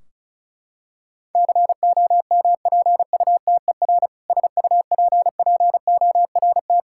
Morse code, heard as a keyed tone, about 36 wpm, 700 Hz. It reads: COMPUTER SUPPORT
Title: Practice QSO - Exchange 3 - More Info - 35wpm